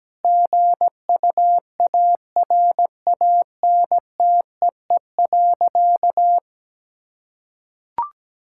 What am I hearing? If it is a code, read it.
Morse code, heard as a keyed tone, about 17 wpm, 700 Hz. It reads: GUARANTEE.
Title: Sentences from Top 1600 Words 17wpm